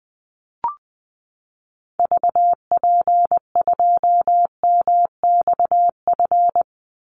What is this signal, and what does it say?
Morse code, heard as a keyed tone, about 20 wpm, 700 Hz. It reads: VP2MXF
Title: Call Signs Repeated 3x - 20wpm